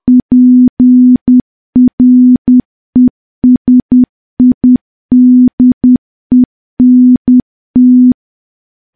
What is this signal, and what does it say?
Morse code, heard as a keyed tone, about 10 wpm, 250 Hz. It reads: PRESIDENT
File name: morse / morse-training-10WPM